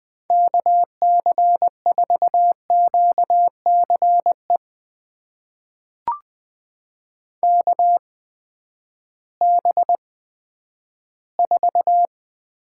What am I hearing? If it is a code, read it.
Morse code, heard as a keyed tone, about 20 wpm, spaced as for 11 wpm, 700 Hz. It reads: KC4QCE KB4
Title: USA Call Signs - 6 Characters - 8x Character Spacing - 20wpm